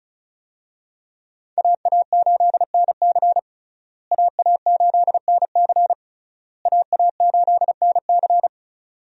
Morse code, heard as a keyed tone, about 34 wpm, 700 Hz. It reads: AA8NC AA8NC AA8NC
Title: USA Call Signs - 5 Characters - Repeated 3x - 35wpm